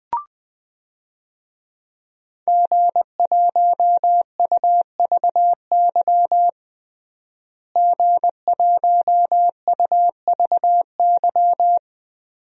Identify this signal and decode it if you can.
Morse code, heard as a keyed tone, about 19 wpm, 700 Hz. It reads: G1UVY G1UVY
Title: DXCC-100 Call Signs - 5 Characters - Repeated 2x - 20wpm